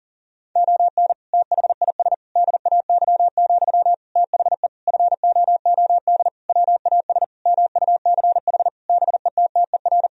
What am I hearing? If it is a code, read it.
Morse code, heard as a keyed tone, about 40 wpm, 700 Hz. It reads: ON THIS DAY, THE FOOD WAS MUCH BETTER
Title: Sentences from Top 1000 Words - Review Part 2 - 40wpm